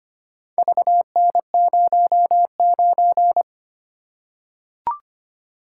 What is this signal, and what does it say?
Morse code, heard as a keyed tone, about 25 wpm, 700 Hz. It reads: VN09
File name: practice / 25wpm / Four Letter-Number Combinations - 25wpm